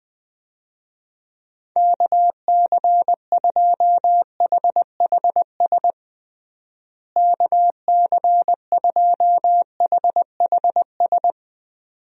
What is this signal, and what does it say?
Morse code, heard as a keyed tone, about 20 wpm, 700 Hz. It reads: KC2HHS KC2HHS
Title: USA Call Signs - 6 Characters - Repeated 2x - 20wpm